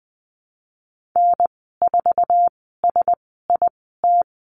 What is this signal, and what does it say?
Morse code, heard as a keyed tone, about 20 wpm, spaced as for 15 wpm, 700 Hz. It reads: N4SIT
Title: 088 Lesson 15g - Number 5 - Call Signs 20wpm